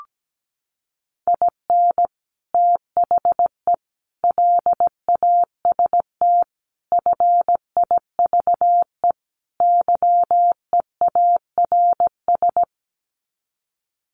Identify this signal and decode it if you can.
Morse code, heard as a keyed tone, about 17 wpm, 700 Hz. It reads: IN THE LAST FIVE YEARS
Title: Sets of 5 Words Encore 17wpm